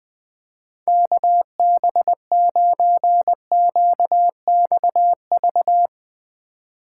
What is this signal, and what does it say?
Morse code, heard as a keyed tone, about 20 wpm, 700 Hz. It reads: KB9QXV
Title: Speed Racing - USA 2x3 Call Signs 17wpm